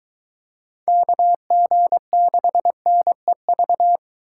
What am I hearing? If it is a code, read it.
Morse code, heard as a keyed tone, about 22 wpm, 700 Hz. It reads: KG6NEV